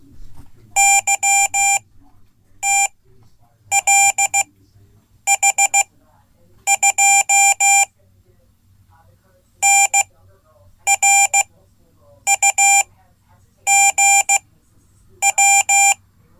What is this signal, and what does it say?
Morse code, heard as a keyed tone, about 16 wpm, spaced as for 8 wpm, 790 Hz. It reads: YTLH2 NRUGW
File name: Code / 5min-c